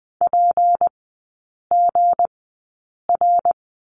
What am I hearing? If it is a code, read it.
Morse code, heard as a keyed tone, about 20 wpm, spaced as for 8 wpm, 700 Hz. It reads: PGR
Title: 178 Lesson 29e - Slash - Three Characters 20wpm